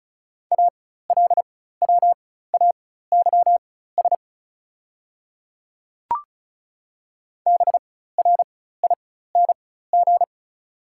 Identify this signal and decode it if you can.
Morse code, heard as a keyed tone, about 34 wpm, spaced as for 16 wpm, 690 Hz. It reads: ALWAYS BRING
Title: 4x Character Spacing - Top 500 Words - 35wpm